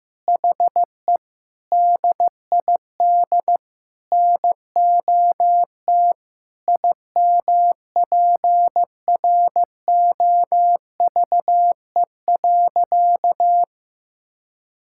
Morse code, heard as a keyed tone, about 15 wpm, 700 Hz. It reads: HE DID NOT IMPROVE.
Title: Sentences from Top 1000 Words - Review Part 5 - 15wpm